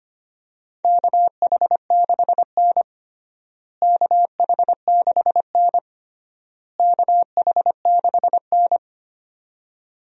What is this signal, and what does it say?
Morse code, heard as a keyed tone, about 25 wpm, 700 Hz. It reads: KH6N KH6N KH6N
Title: DXCC-100 Call Signs - 4 Characters - Repeated 3x - 25wpm